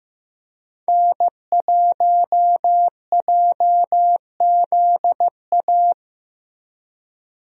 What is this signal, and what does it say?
Morse code, heard as a keyed tone, about 15 wpm, 700 Hz. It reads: N1JZA
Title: USA Call Signs - 5 Characters - Repeated 2x - 15wpm